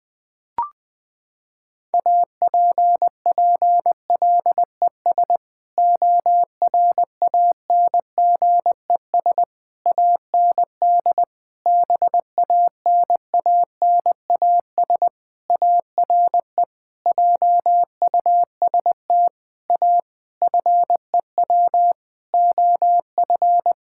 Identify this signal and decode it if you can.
Morse code, heard as a keyed tone, about 20 wpm, 700 Hz. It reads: APPLES ORANGES AND BANANAS ARE JUST A FEW OF
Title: Sentences with 6-Word Phrases - 20wpm